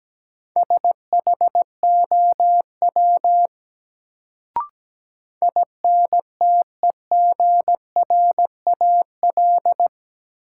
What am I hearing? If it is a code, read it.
Morse code, heard as a keyed tone, about 17 wpm, 700 Hz. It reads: SHOW INTEGRAL
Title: Sets of 5 Words - New Word Drill - 17wpm